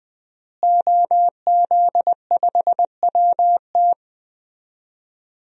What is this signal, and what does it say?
Morse code, heard as a keyed tone, about 20 wpm, 700 Hz. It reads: OZ5WT
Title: DXCC-100 Call Signs - 5 Characters - Repeated 3x - 20wpm